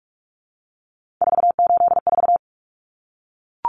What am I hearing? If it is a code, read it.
Morse code, heard as a keyed tone, about 44 wpm, 700 Hz. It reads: 484
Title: Numbers - 3-Digit 45wpm